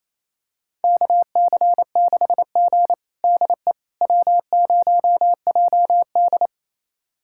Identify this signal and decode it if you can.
Morse code, heard as a keyed tone, about 28 wpm, 700 Hz. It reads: KC6G DE W0JD
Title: USA Callsign DE Callsign - 4 Characters - 28wpm